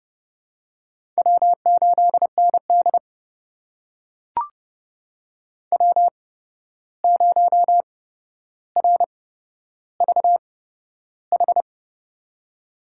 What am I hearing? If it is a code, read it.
Morse code, heard as a keyed tone, about 31 wpm, spaced as for 12 wpm, 700 Hz. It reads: W8ND W0RVH
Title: USA Call Signs - 8x Character Spacing - 30wpm